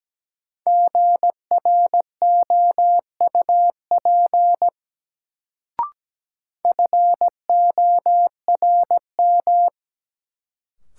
Morse code, heard as a keyed tone, about 17 wpm, 700 Hz. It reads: GROUP FORM
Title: Sentences from Top 1000 Words - Review Part 1 - New Word Drill - 17wpm